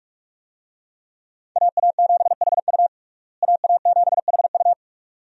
Morse code, heard as a keyed tone, about 41 wpm, 700 Hz. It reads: AA7HU AA7HU
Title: USA Call Signs - 5 Characters - Repeated 2x - 45wpm